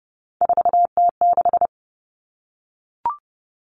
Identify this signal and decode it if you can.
Morse code, heard as a keyed tone, about 30 wpm, 700 Hz. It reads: 4T6
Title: All Cut Numbers - 3-Digit 30wpm